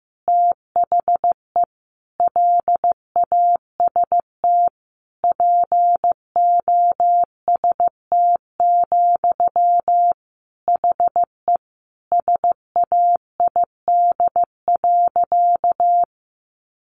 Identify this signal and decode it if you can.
Morse code, heard as a keyed tone, about 15 wpm, 700 Hz. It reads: THE LAST POST, HE SAID.
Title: Sentences from Top 1000 Words 15wpm